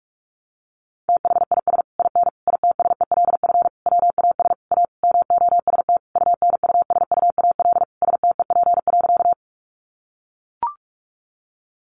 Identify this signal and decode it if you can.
Morse code, heard as a keyed tone, about 44 wpm, 700 Hz. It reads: THIS IN ITSELF WAS A MOST UNUSUAL STEP.
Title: Sentences from Top 1400 Words 45wpm